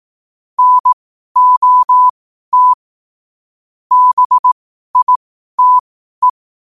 Morse code, heard as a keyed tone, about 18 wpm, spaced as for 12 wpm, 1000 Hz. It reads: NOT BITE
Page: 12 wpm morse code quotes